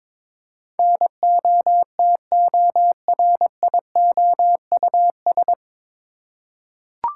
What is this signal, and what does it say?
Morse code, heard as a keyed tone, about 21 wpm, 700 Hz. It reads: NOTORIOUS
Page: Morse Code Ninja practice 22wpm